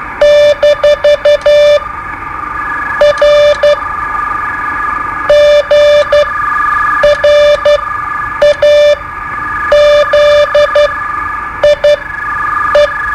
Morse code, heard as a keyed tone, about 12 wpm, spaced as for 7 wpm, 590 Hz. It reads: - R GRAZIE
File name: qso-01